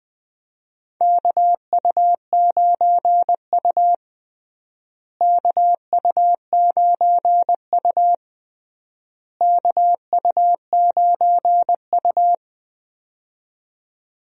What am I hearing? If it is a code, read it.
Morse code, heard as a keyed tone, about 21 wpm, 700 Hz. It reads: KU9U KU9U KU9U
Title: USA Call Signs - 4 Characters - Repeated 3x - 20wpm